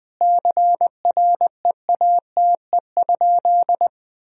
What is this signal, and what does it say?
Morse code, heard as a keyed tone, about 20 wpm, 700 Hz. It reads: CREATE?
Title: Sentences from Top 1000 Words - Review Part 3 - 20wpm